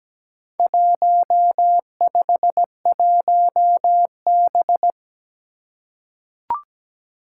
Morse code, heard as a keyed tone, about 17 wpm, 700 Hz. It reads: 151B
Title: Four Challenging Letter-Number Combinations 17wpm